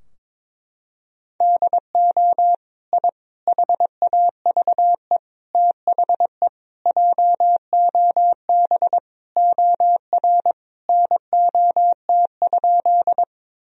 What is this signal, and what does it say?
Morse code, heard as a keyed tone, about 22 wpm, 700 Hz. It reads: DO I HAVE THE JOB OR NOT?
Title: Sentences from Top 1000 Words - Review Part 3 - 22wpm